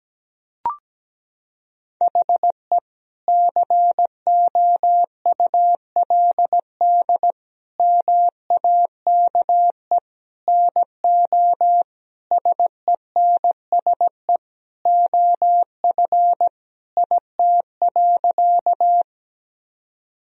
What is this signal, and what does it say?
Morse code, heard as a keyed tone, about 17 wpm, 700 Hz. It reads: HE COULD MAKE NO SENSE OF IT.
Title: Sentences from Top 1000 Words - Review Part 4 - 17wpm